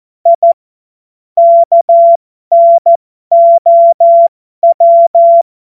Morse code, heard as a keyed tone, about 14 wpm, spaced as for 12 wpm, 680 Hz. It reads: I KNOW
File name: Morsekurs / Witze